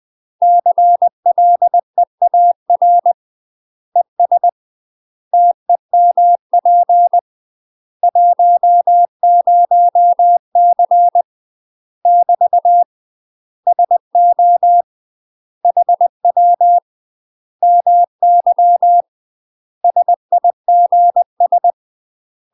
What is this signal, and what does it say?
Morse code, heard as a keyed tone, about 20 wpm, 700 Hz. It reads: CLEAR ES TEMP 10C <BT> SO HW MY SIGS
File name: UK MASTER QSOs TRAINING / 20WPM / UK MASTER QSOs 0030